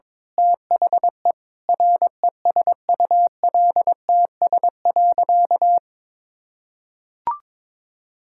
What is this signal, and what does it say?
Morse code, heard as a keyed tone, about 22 wpm, 700 Hz. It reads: THE RESULTS.
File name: Sentences with 7-Word Phrases - 22wpm